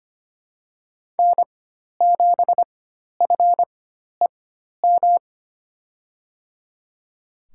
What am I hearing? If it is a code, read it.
Morse code, heard as a keyed tone, about 25 wpm, spaced as for 12 wpm, 700 Hz. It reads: N7FEM
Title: USA Call Signs - 4x Character Spacing - 25wpm